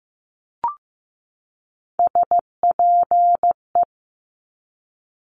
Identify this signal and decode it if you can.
Morse code, heard as a keyed tone, about 15 wpm, 700 Hz. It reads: SPE
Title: Top Three Letter Combinations in Words 15wpm